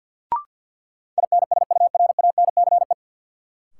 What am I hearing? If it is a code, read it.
Morse code, heard as a keyed tone, about 53 wpm, 700 Hz. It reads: INSURANCE
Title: Sentences from Top 1100 Words - New Word Drill - 50wpm